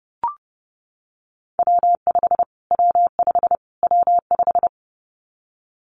Morse code, heard as a keyed tone, about 28 wpm, 690 Hz. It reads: W5 W5 W5
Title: Callsign Prefixes for Top 25 DXCC - Repeated 3x - 30wpm